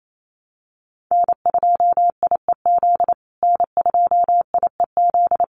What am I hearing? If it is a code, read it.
Morse code, heard as a keyed tone, about 28 wpm, 700 Hz. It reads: N2IEZ N2IEZ
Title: USA Call Signs Repeated 3x 28wpm